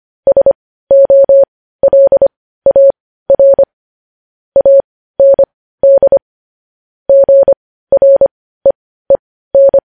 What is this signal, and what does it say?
Morse code, heard as a keyed tone, about 25 wpm, spaced as for 15 wpm, 550 Hz. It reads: SOLAR AND GREEN